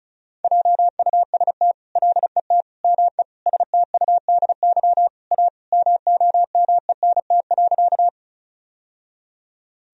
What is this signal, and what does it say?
Morse code, heard as a keyed tone, about 34 wpm, 700 Hz. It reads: JUST LET ME STUDY A MOMENT.
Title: Sentences from Top 1000 Words - Review Part 1 - 35wpm